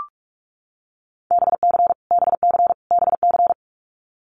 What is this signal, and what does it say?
Morse code, heard as a keyed tone, about 44 wpm, 700 Hz. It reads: BC BC BC
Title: Canadian Provinces and Territories - Repeated 3x - 45wpm